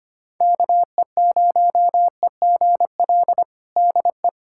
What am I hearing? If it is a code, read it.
Morse code, heard as a keyed tone, about 25 wpm, 700 Hz. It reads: KE0EGL DE
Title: USA Callsign DE Callsign 25wpm